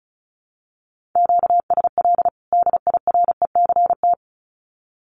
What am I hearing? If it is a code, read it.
Morse code, heard as a keyed tone, about 34 wpm, 700 Hz. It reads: QSL DIRECT
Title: Common QSO Phrases 35wpm